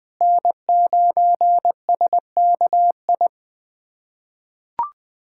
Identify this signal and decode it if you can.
Morse code, heard as a keyed tone, about 20 wpm, 700 Hz. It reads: N9SKI
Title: USA Callsign DE Callsign - 5 Characters - 20wpm